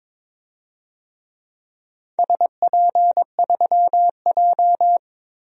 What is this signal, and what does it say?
Morse code, heard as a keyed tone, about 22 wpm, 700 Hz. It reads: SP3J
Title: DXCC-100 Call Signs - 4 Characters - Repeated 3x - 22wpm